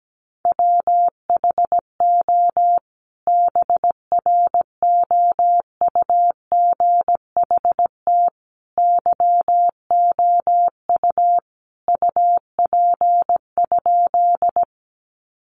Morse code, heard as a keyed tone, about 17 wpm, 700 Hz. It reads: WHO BROUGHT YOU UP?
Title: Sentences from Top 500 Words 17wpm